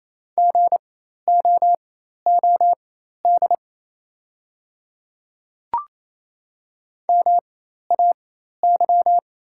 Morse code, heard as a keyed tone, about 27 wpm, spaced as for 13 wpm, 700 Hz. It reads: GOOD MAY
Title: 4x Character Spacing - Top 200 Words - 28wpm